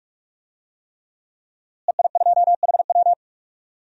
Morse code, heard as a keyed tone, about 44 wpm, 700 Hz. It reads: EI2HW
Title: DXCC-100 Call Signs - 5 Characters - Repeated 3x - 45wpm